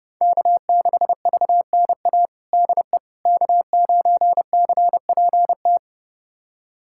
Morse code, heard as a keyed tone, about 32 wpm, 700 Hz. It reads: K6VNA DE K9CPT
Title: USA Callsign DE Callsign - 5 Characters - 30wpm